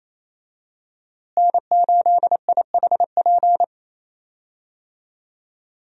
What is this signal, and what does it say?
Morse code, heard as a keyed tone, about 28 wpm, 700 Hz. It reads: N8IHP
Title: USA Call Signs - 5 Characters - Repeated 3x - 28wpm